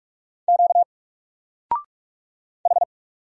Morse code, heard as a keyed tone, about 44 wpm, 700 Hz. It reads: Q H
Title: Single Letter-Number 45wpm